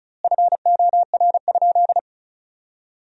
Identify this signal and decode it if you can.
Morse code, heard as a keyed tone, about 36 wpm, 700 Hz. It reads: FOR?
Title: 4x Word-Spacing - Sentences from Top 200 Words - 35wpm